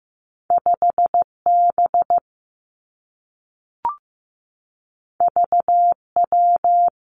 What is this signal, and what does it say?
Morse code, heard as a keyed tone, about 15 wpm, 700 Hz. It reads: 5 B V W
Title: Two Challenging Letter-Number Combinations 15wpm